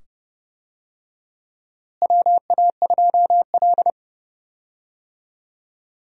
Morse code, heard as a keyed tone, about 30 wpm, 700 Hz. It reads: WA2L
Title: USA Call Signs - 4 Characters - Repeated 2x - 30wpm